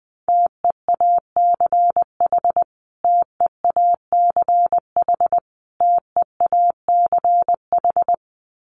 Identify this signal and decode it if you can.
Morse code, heard as a keyed tone, about 19 wpm, 700 Hz. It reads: TEACH TEACH TEACH
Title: Top 500 Words Repeated 20wpm